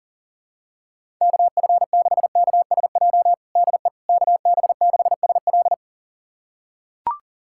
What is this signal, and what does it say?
Morse code, heard as a keyed tone, about 40 wpm, 700 Hz. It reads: KF6KSJ DE KB6SL
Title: USA Callsign DE Callsign 40wpm